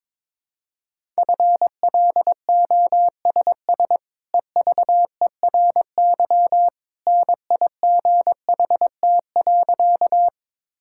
Morse code, heard as a keyed tone, about 22 wpm, 700 Hz. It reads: FLOSS EVERY NIGHT.
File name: practice / 22wpm / Easy Sentences with Hints - 22wpm